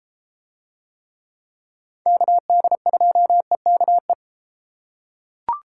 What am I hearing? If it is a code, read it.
Morse code, heard as a keyed tone, about 32 wpm, 700 Hz. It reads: KD2EKE